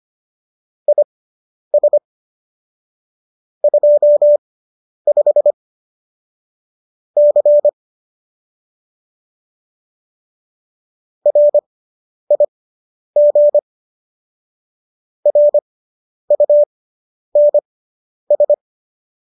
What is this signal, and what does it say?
Morse code, heard as a keyed tone, about 25 wpm, spaced as for 10 wpm, 590 Hz. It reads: IS 25 C RIG RUNS